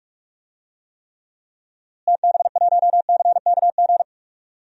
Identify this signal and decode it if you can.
Morse code, heard as a keyed tone, about 46 wpm, 700 Hz. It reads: TB1CKG
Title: DXCC-100 Call Signs - 6 Characters - Repeated 3x - 45wpm